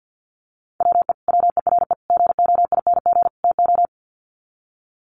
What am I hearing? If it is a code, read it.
Morse code, heard as a keyed tone, about 50 wpm, 700 Hz. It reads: WE WERE GOING TO
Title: Sets of 4 Words 50wpm